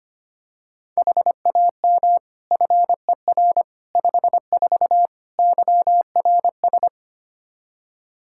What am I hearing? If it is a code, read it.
Morse code, heard as a keyed tone, about 25 wpm, 700 Hz. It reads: HAM FER 54 YRS
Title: Practice QSO - Exchange 3 - More Info - 25wpm